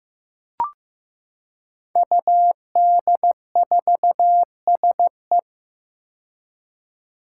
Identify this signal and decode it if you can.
Morse code, heard as a keyed tone, about 15 wpm, 700 Hz. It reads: UD4SE
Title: Five Challenging Letter-Number Combinations 15wpm